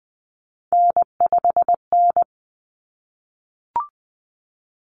Morse code, heard as a keyed tone, about 20 wpm, 700 Hz. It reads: N5N
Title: Common Cut Numbers - 3-Digit 20wpm